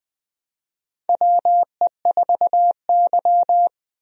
Morse code, heard as a keyed tone, about 20 wpm, 700 Hz. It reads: WE4Y
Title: USA Call Signs - 4 Characters - 20wpm